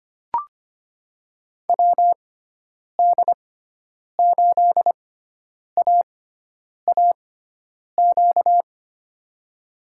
Morse code, heard as a keyed tone, about 25 wpm, spaced as for 9 wpm, 700 Hz. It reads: WD8AAQ
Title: USA Call Signs - 6 Characters - 6x Character Spacing - 25wpm